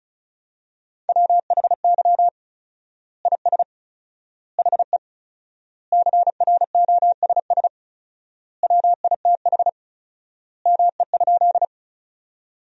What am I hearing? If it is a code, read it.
Morse code, heard as a keyed tone, about 36 wpm, 700 Hz. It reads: WHY IS HE CROSS WITH ME?
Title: 4x Word-Spacing - Sentences from Top 300 Words - 35wpm